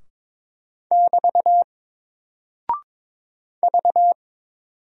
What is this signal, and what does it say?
Morse code, heard as a keyed tone, about 22 wpm, 700 Hz. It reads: <BT> V
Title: Single Character (Letter:Number:Punctuation:Prosign) 22wpm